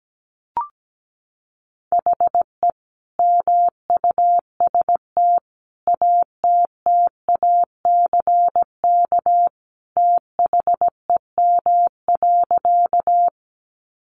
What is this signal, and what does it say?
Morse code, heard as a keyed tone, about 17 wpm, 700 Hz. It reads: HE MUST ATTACK THEM.
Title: Sentences from Top 1500 Words 17wpm